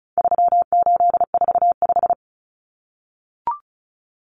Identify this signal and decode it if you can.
Morse code, heard as a keyed tone, about 34 wpm, 700 Hz. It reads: 3845
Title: Common Cut Numbers - 4-Digit 35wpm